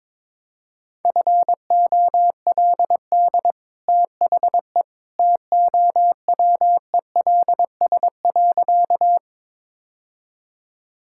Morse code, heard as a keyed tone, about 22 wpm, 700 Hz. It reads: FOLD THE TOWELS.
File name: practice / 22wpm / Easy Sentences with Hints - 22wpm